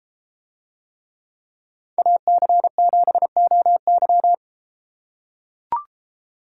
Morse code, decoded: AC7OY

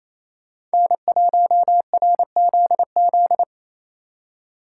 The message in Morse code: N1RZZ

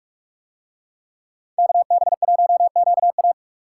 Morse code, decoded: KB1QA

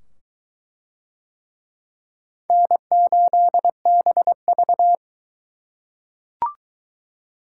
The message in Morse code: N8BV

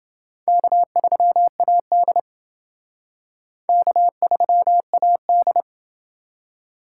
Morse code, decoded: K3AD K3AD

